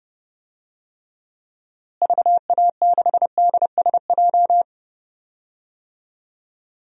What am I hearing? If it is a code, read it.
Morse code, heard as a keyed tone, about 31 wpm, 700 Hz. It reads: VA6DSJ
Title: DXCC-100 Call Signs - 6 Characters - Repeated 2x - 30wpm